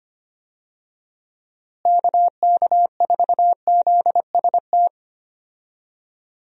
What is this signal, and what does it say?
Morse code, heard as a keyed tone, about 25 wpm, 700 Hz. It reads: KK4ZST
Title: USA Call Signs - 6 Characters - Repeated 3x - 25wpm